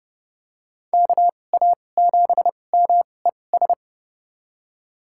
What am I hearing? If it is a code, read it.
Morse code, heard as a keyed tone, about 30 wpm, spaced as for 22 wpm, 700 Hz. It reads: KA7MES
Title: USA Call Signs - 6 Characters - 2x Character Spacing - 30wpm